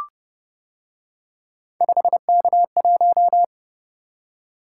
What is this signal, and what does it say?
Morse code, heard as a keyed tone, about 29 wpm, 700 Hz. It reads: 5K1